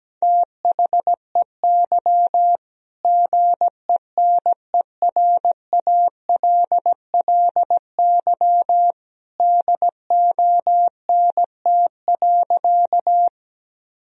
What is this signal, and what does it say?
Morse code, heard as a keyed tone, about 17 wpm, 700 Hz. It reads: THEY GENERALLY DONT.
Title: Sentences from Top 1000 Words - Review Part 4 - 17wpm